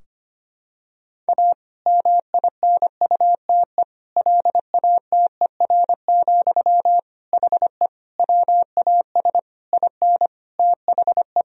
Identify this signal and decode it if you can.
Morse code, heard as a keyed tone, about 25 wpm, 700 Hz. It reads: A MINUTE LATER, HE WAS IN THE
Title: Sentences from Top 1000 Words - Review Part 4 - 25wpm